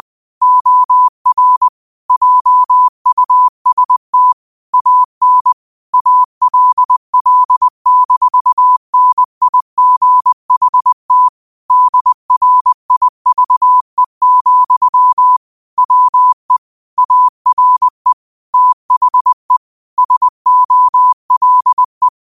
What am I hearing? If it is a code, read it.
Morse code, heard as a keyed tone, about 20 wpm, 1000 Hz. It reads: OR JUST AN ALL-NIGHT DRIVE, WE ARE THE SOLE
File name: qotdfiles / 20wpm